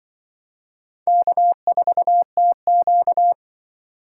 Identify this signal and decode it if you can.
Morse code, heard as a keyed tone, about 24 wpm, 700 Hz. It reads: K4TQ
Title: Speed Racing - USA 1x2 Call Signs 20wpm